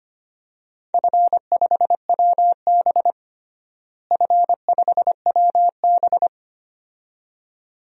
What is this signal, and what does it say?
Morse code, heard as a keyed tone, about 25 wpm, 700 Hz. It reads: F5WB F5WB
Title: DXCC-100 Call Signs - 4 Characters - Repeated 2x - 25wpm